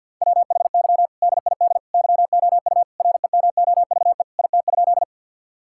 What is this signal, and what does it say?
Morse code, heard as a keyed tone, about 50 wpm, 700 Hz. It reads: WHY DID YOU REMOVE IT?
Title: Sentences from Top 1000 Words - Review Part 5 - 50wpm